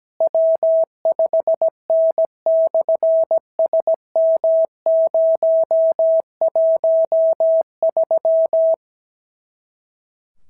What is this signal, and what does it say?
Morse code, heard as a keyed tone, about 17 wpm, 650 Hz. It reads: W5N/SM013
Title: SOTA QSO Practice 17wpm